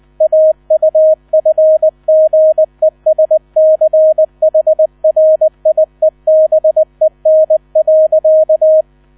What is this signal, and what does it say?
Morse code, heard as a keyed tone, about 20 wpm, 630 Hz, with radio noise. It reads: AUFGESCHRIEBEN.
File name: 000 Vorwort 20wpm